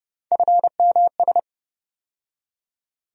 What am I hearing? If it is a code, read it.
Morse code, heard as a keyed tone, about 31 wpm, 700 Hz. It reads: FMS
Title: USA Call Sign Postfixes - 3 Letters 30wpm